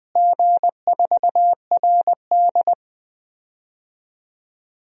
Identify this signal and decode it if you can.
Morse code, heard as a keyed tone, about 20 wpm, 700 Hz. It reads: G4RD
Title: DXCC-100 Call Signs - 4 Characters - Repeated 3x - 20wpm